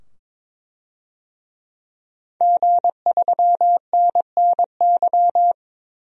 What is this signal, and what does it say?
Morse code, heard as a keyed tone, about 22 wpm, 700 Hz. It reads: G3NNY
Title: DXCC-100 Call Signs - 5 Characters - 22wpm